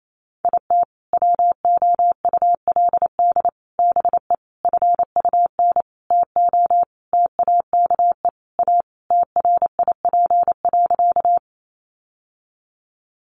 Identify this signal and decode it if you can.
Morse code, heard as a keyed tone, about 28 wpm, 700 Hz. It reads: IT WOULD BE FUN TO TAKE A TRIP.
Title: Sentences from Top 1300 Words 28wpm